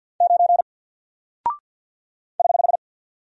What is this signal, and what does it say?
Morse code, heard as a keyed tone, about 49 wpm, 700 Hz. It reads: 9 <HH>
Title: Single Character (Letter:Number:Punctuation:Prosign) Alt 50wpm